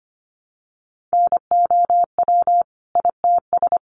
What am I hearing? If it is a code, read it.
Morse code, heard as a keyed tone, about 25 wpm, 700 Hz. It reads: NOW ITS
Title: Sentences from Top 600 Words 25wpm